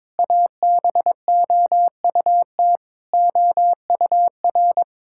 Morse code, heard as a keyed tone, about 21 wpm, 700 Hz. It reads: ABOUT OUR
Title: Sentences from Top 1000 Words - Review Part 6 - 22wpm